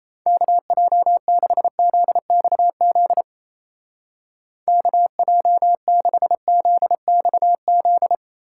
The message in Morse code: KJ6ZXZ KJ6ZXZ

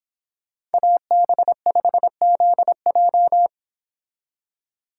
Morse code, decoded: AB5ZJ

